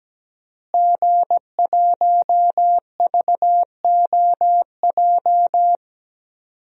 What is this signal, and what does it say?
Morse code, heard as a keyed tone, about 17 wpm, 700 Hz. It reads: G1VOJ
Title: DXCC-100 Call Signs - 5 Characters - Repeated 3x - 17wpm